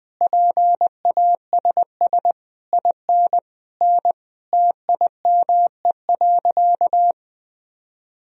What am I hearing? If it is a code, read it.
Morse code, heard as a keyed tone, about 21 wpm, 700 Hz. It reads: PASS IN N TIME.
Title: Sentences from Top 400 Words and Abbreviations 20wpm